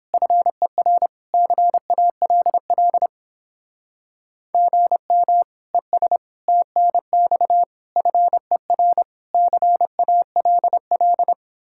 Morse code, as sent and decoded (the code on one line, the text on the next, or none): ..-. .-. / -.-. .- .-.. .-.. / --. -- / . ... / - -. -..- / ..-. .-. / -.-. .- .-.. .-..
FER CALL GM ES TNX FER CALL